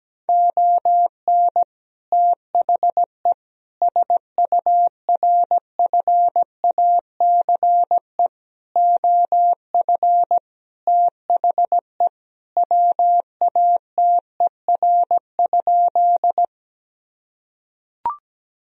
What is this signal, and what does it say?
Morse code, heard as a keyed tone, about 17 wpm, 700 Hz. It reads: ON THE SURFACE OF THE WATER?
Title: Sentences from Top 1000 Words - Review Part 3 - 17wpm